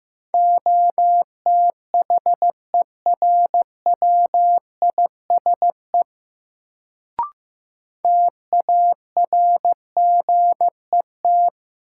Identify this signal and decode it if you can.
Morse code, heard as a keyed tone, about 15 wpm, 700 Hz. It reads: OTHERWISE TARGET
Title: Sentences from Top 1200 Words - New Word Drill Alt - 15wpm